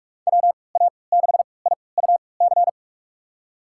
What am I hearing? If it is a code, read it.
Morse code, heard as a keyed tone, about 46 wpm, spaced as for 26 wpm, 700 Hz. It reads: WA6IUC
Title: USA Call Signs - 6 Characters - 3x Character Spacing - 45wpm